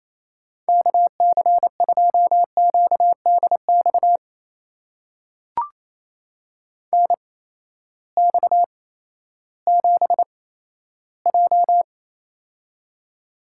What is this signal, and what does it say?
Morse code, heard as a keyed tone, about 28 wpm, spaced as for 14 wpm, 700 Hz. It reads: KC2QDX NX7J
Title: USA Call Signs - 8x Character Spacing - 28wpm